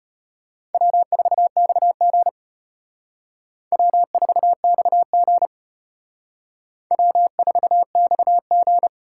Morse code, heard as a keyed tone, about 32 wpm, 700 Hz. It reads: W4XG W4XG W4XG